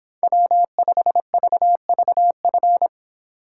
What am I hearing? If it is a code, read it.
Morse code, heard as a keyed tone, about 23 wpm, 700 Hz. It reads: W5VVF